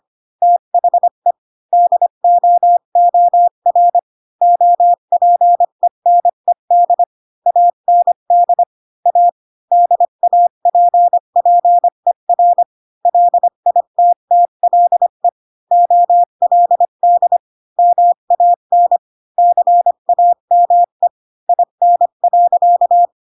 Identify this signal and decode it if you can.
Morse code, heard as a keyed tone, about 25 wpm, 700 Hz. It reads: THE DOOR OPENED AND A DAPPER LITTLE OLD MAN CAME IN.
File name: Pinocchio / Pinocchio ch2 20 25 ik1hge